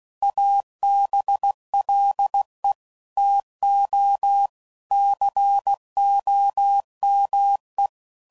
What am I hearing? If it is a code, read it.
Morse code, heard as a keyed tone, about 16 wpm, 800 Hz. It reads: ABLE TO COME
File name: CW Code Groups @15WPM / QSOs / WA7QOX